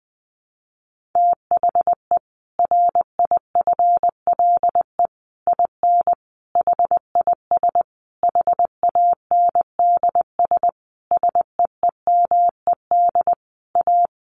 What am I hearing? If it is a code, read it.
Morse code, heard as a keyed tone, about 20 wpm, 700 Hz. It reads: THE RIFLE IN HIS HANDS SEEMED A
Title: A Sound Of Thunder 20wpm